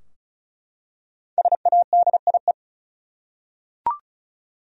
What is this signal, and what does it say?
Morse code, heard as a keyed tone, about 34 wpm, 700 Hz. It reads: SADIE
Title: USA Common Names - Max 5 Letters - 35wpm